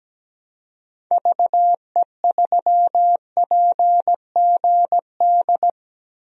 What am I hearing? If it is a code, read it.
Morse code, heard as a keyed tone, about 17 wpm, 700 Hz. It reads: VE3PGD
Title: DXCC-100 Call Signs - 6 Characters - 17wpm